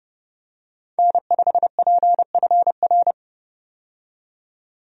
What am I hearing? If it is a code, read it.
Morse code, heard as a keyed tone, about 30 wpm, 700 Hz. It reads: N5PFR
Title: USA Call Signs - 5 Characters - 30wpm